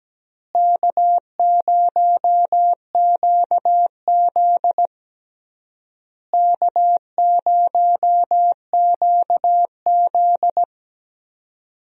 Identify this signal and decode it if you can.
Morse code, heard as a keyed tone, about 17 wpm, 700 Hz. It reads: K0QZ K0QZ
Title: USA Call Signs - 4 Characters - Repeated 3x - 17wpm